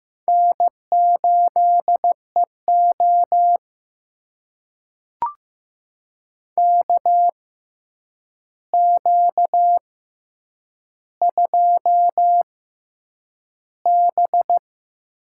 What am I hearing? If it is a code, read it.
Morse code, heard as a keyed tone, about 15 wpm, spaced as for 8 wpm, 700 Hz. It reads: N8EO KQ2B
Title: USA Call Signs - 4 Characters - 6x Character Spacing - 15wpm